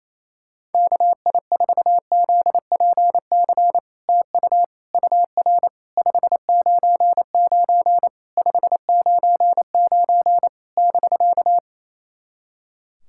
Morse code, heard as a keyed tone, about 28 wpm, 690 Hz. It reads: KI4ZPC TU UR 599 599 <BK>
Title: SOTA QSO Practice 28wpm